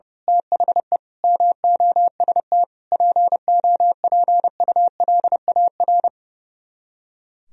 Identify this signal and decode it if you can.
Morse code, heard as a keyed tone, about 29 wpm, 700 Hz. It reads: THE MOST POPULAR
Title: Speed-Racing - Sets of 5 Words 30wpm